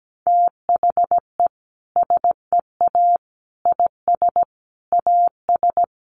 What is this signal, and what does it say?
Morse code, heard as a keyed tone, about 17 wpm, 700 Hz. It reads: THE SEA IS AS
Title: Sentences from Top 1400 Words 17wpm